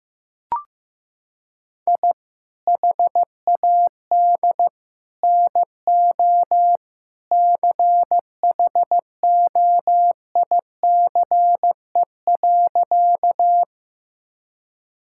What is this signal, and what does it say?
Morse code, heard as a keyed tone, about 15 wpm, 700 Hz. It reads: I HAD NO CHOICE.
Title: Sentences from Top 1000 Words - Review Part 6 - 15wpm